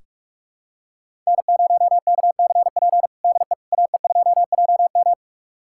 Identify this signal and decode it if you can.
Morse code, heard as a keyed tone, about 46 wpm, 700 Hz. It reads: N0KCP DE AE2JM
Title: USA Callsign DE Callsign 45wpm